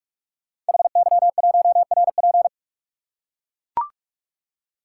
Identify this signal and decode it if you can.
Morse code, heard as a keyed tone, about 44 wpm, 700 Hz. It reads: HY1RP
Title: Code Groups - 5 Letter:Numbers - 45wpm